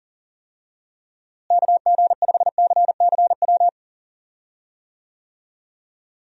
Morse code, heard as a keyed tone, about 40 wpm, 700 Hz. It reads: KG5CCW
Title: USA Call Signs - 6 Characters - Repeated 3x - 40wpm